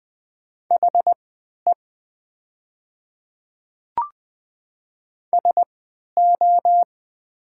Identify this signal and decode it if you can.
Morse code, heard as a keyed tone, about 20 wpm, spaced as for 11 wpm, 690 Hz. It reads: HE SO
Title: 3x Character Spacing - Top 100 Words - 20wpm